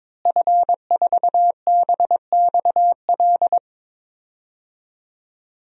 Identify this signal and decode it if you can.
Morse code, heard as a keyed tone, about 21 wpm, 700 Hz. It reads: F4BXL
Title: DXCC-100 Call Signs - 5 Characters - Repeated 3x - 22wpm